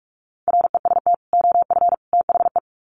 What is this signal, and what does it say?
Morse code, heard as a keyed tone, about 44 wpm, 700 Hz. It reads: REST OF THE
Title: Sets of 3 Words 45wpm